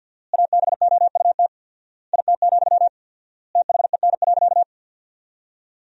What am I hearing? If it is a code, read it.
Morse code, heard as a keyed tone, about 52 wpm, 700 Hz. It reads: ABOUT IT, THEN.
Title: 4x Word-Spacing - Sentences from Top 200 Words - 50wpm